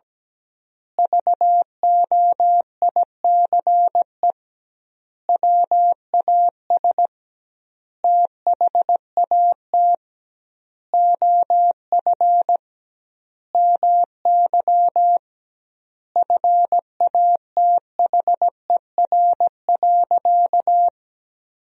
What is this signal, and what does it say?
Morse code, heard as a keyed tone, about 17 wpm, 700 Hz. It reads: VOICE WAS THAT OF MY FATHER.
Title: 2x Word-Spacing - Sentences from Top 500 Words - 17wpm